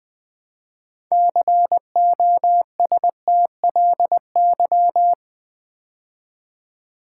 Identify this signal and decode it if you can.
Morse code, heard as a keyed tone, about 20 wpm, 700 Hz. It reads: COSTLY
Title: Build A Word - 3 Letters At A Time - 6-Letter Words - 20wpm